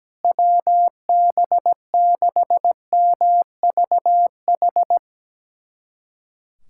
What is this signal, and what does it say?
Morse code, heard as a keyed tone, about 17 wpm, 700 Hz. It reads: WB6MVH
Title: USA Callsign DE Callsign - 6 Characters - 17wpm